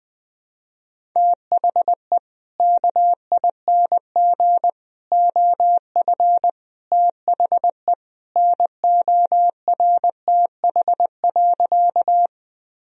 Morse code, heard as a keyed tone, about 21 wpm, 700 Hz. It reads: THE KING OF THE NORTH.
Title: Sentences from Top 1000 Words - Review Part 1 - 20wpm